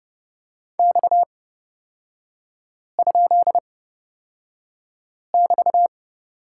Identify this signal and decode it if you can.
Morse code, heard as a keyed tone, about 31 wpm, spaced as for 5 wpm, 690 Hz. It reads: X?<BT>
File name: Lesson 37e - Letter x - Three Characters 30wpm